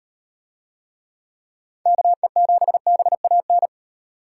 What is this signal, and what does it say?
Morse code, heard as a keyed tone, about 38 wpm, 700 Hz. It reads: KE7BAN